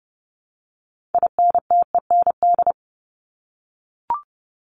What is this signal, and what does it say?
Morse code, heard as a keyed tone, about 30 wpm, 700 Hz. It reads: INTEND
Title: Top 1501-1750 Words 30wpm